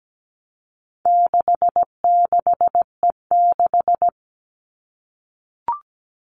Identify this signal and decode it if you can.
Morse code, heard as a keyed tone, about 17 wpm, 700 Hz. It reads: 66E6
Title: All Cut Numbers - 4-Digit 17wpm